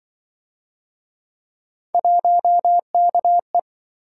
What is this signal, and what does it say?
Morse code, heard as a keyed tone, about 28 wpm, 700 Hz. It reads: 1KE